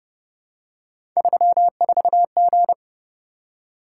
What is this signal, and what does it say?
Morse code, heard as a keyed tone, about 29 wpm, 700 Hz. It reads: 34G